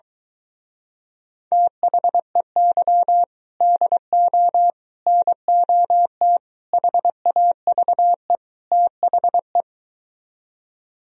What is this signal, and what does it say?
Morse code, heard as a keyed tone, about 24 wpm, 700 Hz. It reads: THEY DO NOT HAVE THE